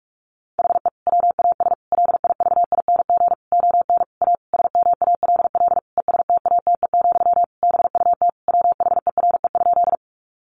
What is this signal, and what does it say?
Morse code, heard as a keyed tone, about 44 wpm, 700 Hz. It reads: HE WAS LIVING ON A SMALL ESTATE, BUT WHERE?
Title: Sentences from Top 1200 Words 45wpm